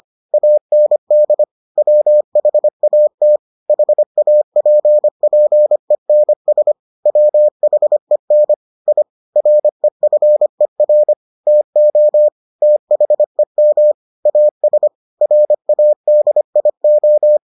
Morse code, heard as a keyed tone, about 25 wpm, 600 Hz. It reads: AND WHAT HAPPENS WHEN I REFER TO THEM AS RADIO